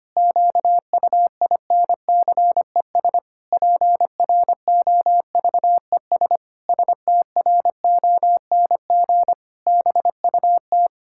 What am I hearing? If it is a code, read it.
Morse code, heard as a keyed tone, about 25 wpm, 700 Hz. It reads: QUINCES PROVES STRONG BUT
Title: Pangram Challenge v2 - 25wpm